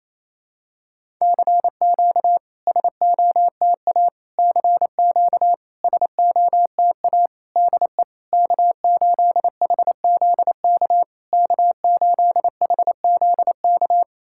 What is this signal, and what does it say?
Morse code, heard as a keyed tone, about 28 wpm, 700 Hz. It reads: CQ SOTA CQ SOTA DE K8HZK K8HZK
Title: SOTA QSO Practice 28wpm